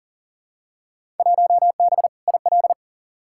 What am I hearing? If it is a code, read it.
Morse code, heard as a keyed tone, about 40 wpm, 700 Hz. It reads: 1B IL
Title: Field Day Exchanges Encore - 40wpm